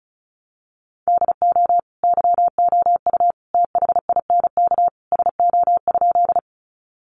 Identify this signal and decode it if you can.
Morse code, heard as a keyed tone, about 34 wpm, 700 Hz. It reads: DO YOU THINK SO?
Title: Sentences from Top 200 Words 35wpm